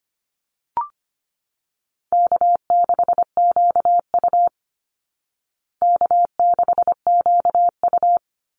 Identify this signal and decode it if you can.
Morse code, heard as a keyed tone, about 25 wpm, 700 Hz. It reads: K6QU K6QU
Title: Call Signs Repeated 2x - 25wpm